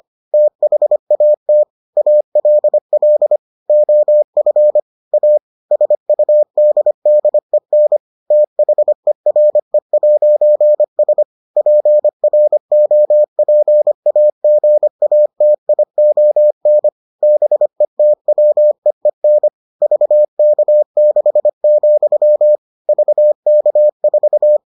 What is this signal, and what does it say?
Morse code, heard as a keyed tone, about 25 wpm, 600 Hz. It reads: THAT ALL OF A SUDDEN THERE'S PROPAGATION BETWEEN VK6, VK4